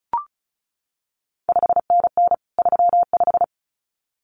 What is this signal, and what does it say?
Morse code, heard as a keyed tone, about 36 wpm, 700 Hz. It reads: 5NN 35
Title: CQ WW DX Contest Exchange Encore - 35wpm